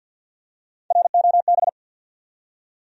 Morse code, heard as a keyed tone, about 46 wpm, 700 Hz. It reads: ROB
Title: USA Common Names - Max 4 Letters - 50wpm